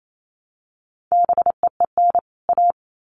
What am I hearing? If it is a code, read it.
Morse code, heard as a keyed tone, about 28 wpm, 700 Hz. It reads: BEEN A
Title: Sets of 2 Words 28wpm